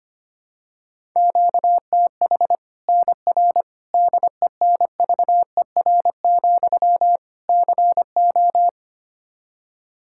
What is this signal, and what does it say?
Morse code, heard as a keyed tone, about 25 wpm, 700 Hz. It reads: QTH NR DENVER, CO
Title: K1USN SST QSO Practice 25wpm